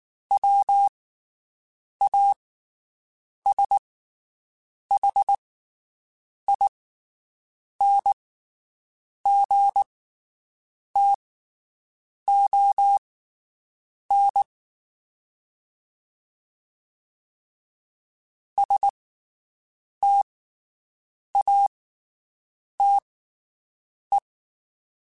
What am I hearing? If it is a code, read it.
Morse code, heard as a keyed tone, about 19 wpm, spaced as for 7 wpm, 800 Hz. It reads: WASHINGTON STATE
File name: (spaced) / QSOs Spaced-mp3s / WD0BGP 1 @18 sp